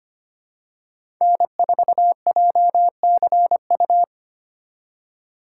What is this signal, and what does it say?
Morse code, heard as a keyed tone, about 25 wpm, 700 Hz. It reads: N4JCU
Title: USA Call Signs - 5 Characters - 2x Character Spacing - 25wpm